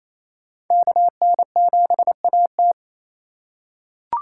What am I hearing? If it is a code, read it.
Morse code, heard as a keyed tone, about 28 wpm, 700 Hz. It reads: KN7AT
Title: USA Call Signs - 3x Character Spacing - 28wpm